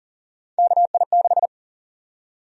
Morse code, heard as a keyed tone, about 40 wpm, 700 Hz. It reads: KI6
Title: USA Call Sign Prefixes - Repeated 2x - 40wpm